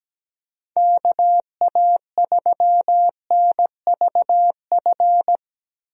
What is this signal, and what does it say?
Morse code, heard as a keyed tone, about 17 wpm, 700 Hz. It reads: KA3NVF
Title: USA Call Signs - 6 Characters - 17wpm